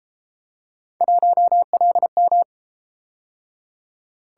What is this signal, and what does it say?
Morse code, heard as a keyed tone, about 30 wpm, 700 Hz. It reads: 1LM